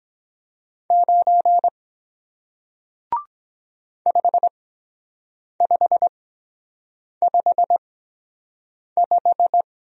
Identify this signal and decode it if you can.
Morse code, heard as a keyed tone, about 23 wpm, spaced as for 7 wpm, 700 Hz. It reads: 9 5555